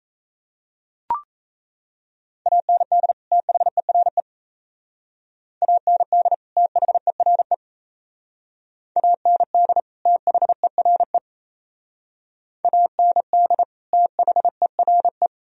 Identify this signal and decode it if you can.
Morse code, heard as a keyed tone, about 35 wpm, 700 Hz. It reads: AND THERE AND THERE AND THERE AND THERE
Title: Speed-Racing - Sets of 2 Words 28wpm